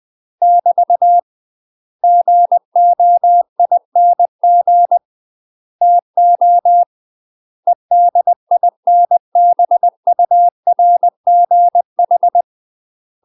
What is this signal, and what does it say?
Morse code, heard as a keyed tone, about 20 wpm, 700 Hz. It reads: <BT> GOING TO EDINBURGH
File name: UK MASTER QSOs TRAINING / 20WPM / UK MASTER QSOs 0051